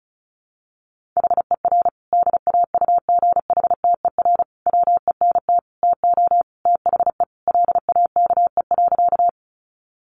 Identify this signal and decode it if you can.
Morse code, heard as a keyed tone, about 34 wpm, 700 Hz. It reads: HER DAUGHTER WENT TO THE LAKE.
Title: Sentences from Top 1500 Words 35wpm